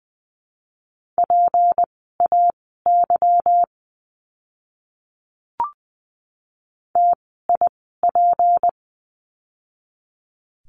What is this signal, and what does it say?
Morse code, heard as a keyed tone, about 20 wpm, spaced as for 14 wpm, 700 Hz. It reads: PAY TIP
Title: 153 Lesson 25f - Letter p - Words 20wpm